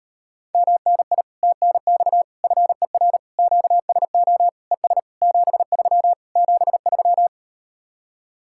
Morse code, heard as a keyed tone, about 38 wpm, 700 Hz. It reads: MNI TNX FER QSO ES 73 73